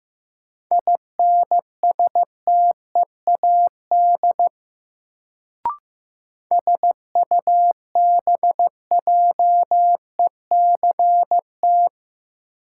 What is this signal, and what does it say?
Morse code, heard as a keyed tone, about 15 wpm, 700 Hz. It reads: INSTEAD SUBJECT
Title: Sentences from Top 800 Words - New Word Drill - 15wpm